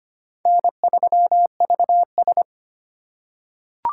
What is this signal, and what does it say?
Morse code, heard as a keyed tone, about 25 wpm, 700 Hz. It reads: N3VS
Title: USA Call Signs - 4 Characters - 3x Character Spacing - 25wpm